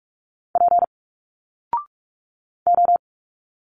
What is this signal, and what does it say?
Morse code, heard as a keyed tone, about 46 wpm, 700 Hz. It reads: P O
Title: Single Letters 45wpm